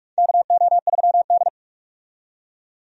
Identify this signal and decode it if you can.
Morse code, heard as a keyed tone, about 46 wpm, 700 Hz. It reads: KO3D
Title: USA Call Signs - 4 Characters - 45wpm